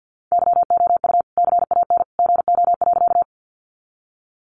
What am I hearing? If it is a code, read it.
Morse code, heard as a keyed tone, about 50 wpm, 700 Hz. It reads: YOU CAN GO.
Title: Sentences from Top 100 Words 50wpm